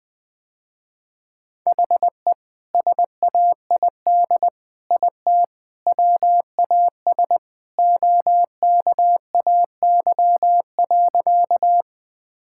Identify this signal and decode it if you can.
Morse code, heard as a keyed tone, about 20 wpm, 700 Hz. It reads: HE SAID IT WAS OKAY.
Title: Sentences with 2-Word Phrases - Spoken Beginning - 20wpm